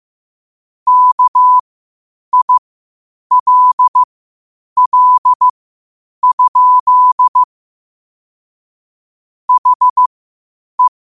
Morse code, heard as a keyed tone, about 15 wpm, spaced as for 8 wpm, 1000 Hz. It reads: KILL? HE
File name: qotdfiles / 8wpm